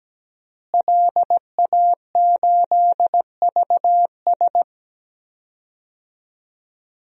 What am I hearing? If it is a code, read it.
Morse code, heard as a keyed tone, about 17 wpm, 700 Hz. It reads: LA8VS
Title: DXCC-100 Call Signs - 5 Characters - Repeated 3x - 17wpm